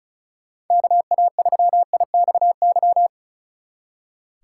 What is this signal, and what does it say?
Morse code, heard as a keyed tone, about 42 wpm, 700 Hz. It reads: KA3IXY
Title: Speed Racing - USA 2x3 Call Signs 35wpm